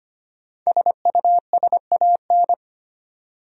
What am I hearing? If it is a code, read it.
Morse code, heard as a keyed tone, about 25 wpm, 700 Hz. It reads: SUSAN
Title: USA Common Names - 25wpm